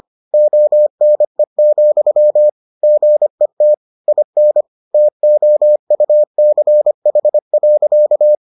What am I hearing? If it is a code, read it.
Morse code, heard as a keyed tone, about 25 wpm, 600 Hz. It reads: ONE, GET IN TOUCH.